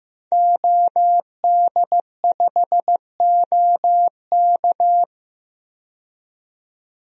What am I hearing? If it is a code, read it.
Morse code, heard as a keyed tone, about 15 wpm, 700 Hz. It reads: OD5OK
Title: DXCC-100 Call Signs - 5 Characters - 15wpm